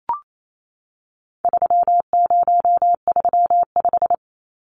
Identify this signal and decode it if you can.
Morse code, heard as a keyed tone, about 28 wpm, 700 Hz. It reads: 3035
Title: Numbers - 4-Digit 28wpm